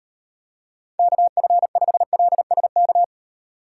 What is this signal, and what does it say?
Morse code, heard as a keyed tone, about 34 wpm, 700 Hz. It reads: KF5LSK